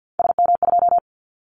Morse code, heard as a keyed tone, about 50 wpm, 700 Hz. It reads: SM2
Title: Callsign Prefixes for Top 25 DXCC - 50wpm